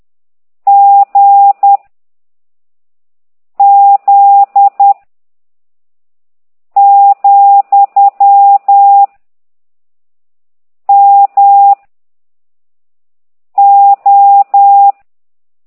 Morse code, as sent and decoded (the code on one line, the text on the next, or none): --. / --.. / --..-- / -- / ---
G Z , M O